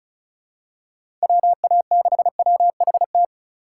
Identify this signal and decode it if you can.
Morse code, heard as a keyed tone, about 34 wpm, 700 Hz. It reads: WA6WHT
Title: USA Call Signs - 2x Character Spacing - 35wpm